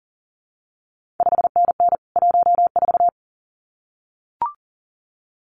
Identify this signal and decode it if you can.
Morse code, heard as a keyed tone, about 39 wpm, 700 Hz. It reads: 5NN 14
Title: CQ WW DX Contest Exchange Encore - 40wpm